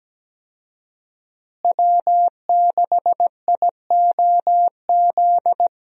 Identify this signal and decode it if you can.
Morse code, heard as a keyed tone, about 17 wpm, 700 Hz. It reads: W6IOZ
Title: USA Call Signs - 5 Characters - Repeated 2x - 17wpm